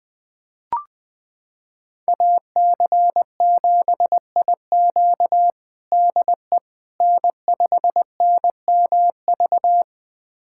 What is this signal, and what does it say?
Morse code, heard as a keyed tone, about 21 wpm, 700 Hz. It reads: AC7IQ DE N5NMV
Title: USA Callsign DE Callsign - 5 Characters - 20wpm